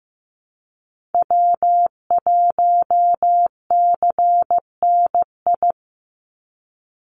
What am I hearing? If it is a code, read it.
Morse code, heard as a keyed tone, about 15 wpm, 700 Hz. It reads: W1CNI
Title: USA Call Signs 15wpm